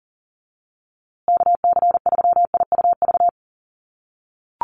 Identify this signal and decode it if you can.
Morse code, heard as a keyed tone, about 40 wpm, 700 Hz. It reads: KC3IUV
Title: USA Call Signs Repeated 2x 40wpm